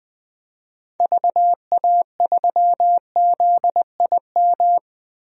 Morse code, heard as a keyed tone, about 20 wpm, 700 Hz. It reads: VA3ZIM
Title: DXCC-100 Call Signs - 6 Characters - 20wpm